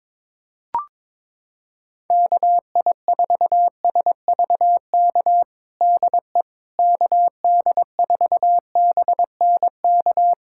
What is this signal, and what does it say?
Morse code, heard as a keyed tone, about 22 wpm, 700 Hz. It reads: KI4SVK DE KD4BNK
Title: USA Callsign DE Callsign - 6 Characters - 22wpm